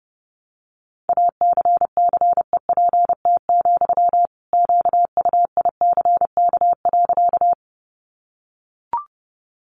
Morse code, heard as a keyed tone, about 30 wpm, 700 Hz. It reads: ACCEPT, QUICK.
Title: Sentences from Top 1300 Words 30wpm